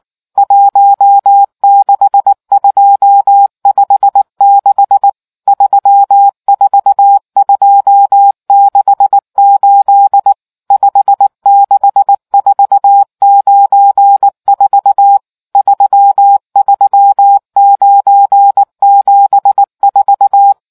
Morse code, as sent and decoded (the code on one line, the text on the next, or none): .---- -.... ..--- ..... -.... / ...-- ....- ..--- -.... ---.. / ..... -.... ....- ----. ....- / ...-- ...-- ----. --... ....-
16256 34268 56494 33974